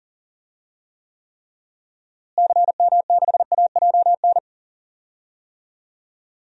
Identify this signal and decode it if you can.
Morse code, heard as a keyed tone, about 40 wpm, 700 Hz. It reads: CM6AJN
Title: DXCC-100 Call Signs - 6 Characters - Repeated 3x - 40wpm